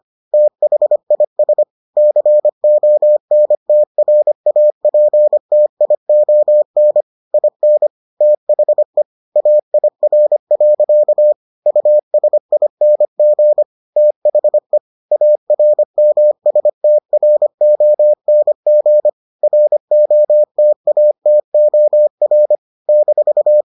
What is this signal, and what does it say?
Morse code, heard as a keyed tone, about 25 wpm, 600 Hz. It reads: THIS CONTRAPTION IN THE AIR. USING THE ARMSTRONG ROTATOR -